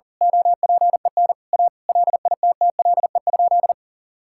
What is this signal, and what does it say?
Morse code, heard as a keyed tone, about 40 wpm, 700 Hz. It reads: OPEN A LITTLE?
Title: Sentences from Top 1600 Words 40wpm